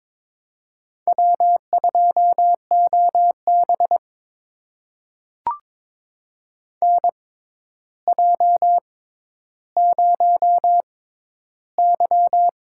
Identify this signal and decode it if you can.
Morse code, heard as a keyed tone, about 22 wpm, spaced as for 11 wpm, 700 Hz. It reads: W2OB NJ0Y
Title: USA Call Signs - 6x Character Spacing - 22wpm